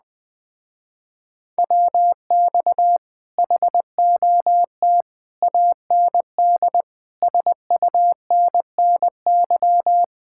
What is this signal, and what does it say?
Morse code, heard as a keyed tone, about 20 wpm, 700 Hz. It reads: WX HOT AND SUNNY
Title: Speed-Racing Common QSO Phrases - 17wpm